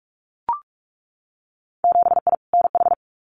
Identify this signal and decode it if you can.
Morse code, heard as a keyed tone, about 44 wpm, 700 Hz. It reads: 7I NH
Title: Winter Field Day Exchanges Encore 45wpm